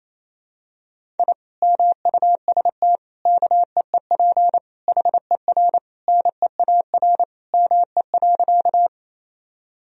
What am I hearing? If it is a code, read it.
Morse code, heard as a keyed tone, about 28 wpm, 700 Hz. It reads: I MUST KEEP HER NEAR ME.
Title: Sentences from Top 1000 Words - Review Part 1 - 28wpm